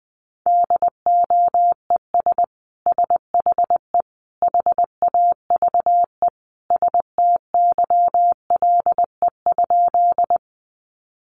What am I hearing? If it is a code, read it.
Morse code, heard as a keyed tone, about 20 wpm, 700 Hz. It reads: DOES SHE HAVE STYLE?
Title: Sentences from Top 800 Words 20wpm